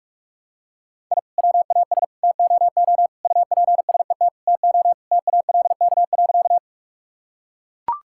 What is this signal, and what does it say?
Morse code, heard as a keyed tone, about 41 wpm, 700 Hz. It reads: I WAS TOO UPSET TO TALK.
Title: Sentences from Top 1000 Words - Review Part 5 - 45wpm